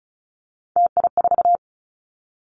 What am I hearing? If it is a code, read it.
Morse code, heard as a keyed tone, about 36 wpm, 700 Hz. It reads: TI4
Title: Callsign Prefixes for Top 100 DXCC - 35wpm